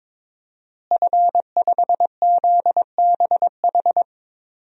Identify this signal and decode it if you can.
Morse code, heard as a keyed tone, about 22 wpm, 700 Hz. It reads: F5ZBH
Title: DXCC-100 Call Signs - 5 Characters - Repeated 2x - 22wpm